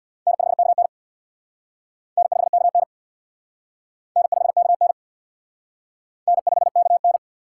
Morse code, heard as a keyed tone, about 65 wpm, 700 Hz. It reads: N5KN N5KN N5KN N5KN